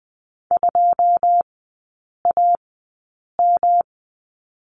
Morse code, heard as a keyed tone, about 20 wpm, spaced as for 8 wpm, 700 Hz. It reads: 2AM
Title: 107 Lesson 18e - Letter m - Three Characters 20wpm